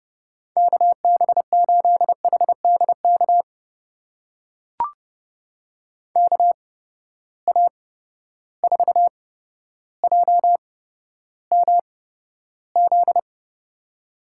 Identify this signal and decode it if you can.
Morse code, heard as a keyed tone, about 30 wpm, spaced as for 13 wpm, 700 Hz. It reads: KB8HDK KA4JMZ
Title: USA Call Signs - 6 Characters - 8x Character Spacing - 30wpm